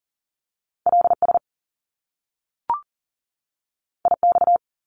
Morse code, heard as a keyed tone, about 40 wpm, 700 Hz. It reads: LS IX
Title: Top Two Letter Combinations in Words 40wpm